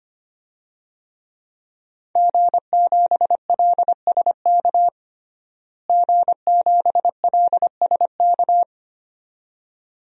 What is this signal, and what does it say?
Morse code, heard as a keyed tone, about 25 wpm, 700 Hz. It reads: G7LSK G7LSK
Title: DXCC-100 Call Signs - 5 Characters - Repeated 2x - 25wpm